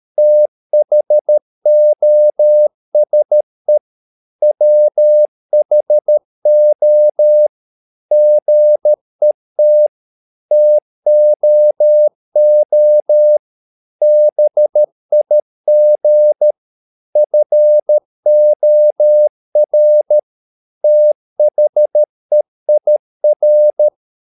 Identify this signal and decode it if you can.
Morse code, heard as a keyed tone, about 13 wpm, 600 Hz. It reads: THOSE WHO GET TOO BIG FOR THEIR
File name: mp3 files / Lexiphiles - 13wpm